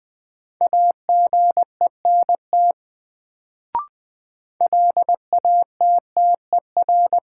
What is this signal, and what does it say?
Morse code, heard as a keyed tone, about 20 wpm, 700 Hz. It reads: AGENT LATTER
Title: Sentences from Top 1600 Words - New Word Drill - 20wpm